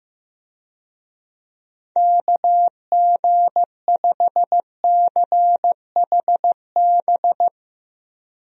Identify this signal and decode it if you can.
Morse code, heard as a keyed tone, about 15 wpm, 700 Hz. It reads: KG5CHB
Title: USA Call Signs - 6 Characters - Repeated 2x - 15wpm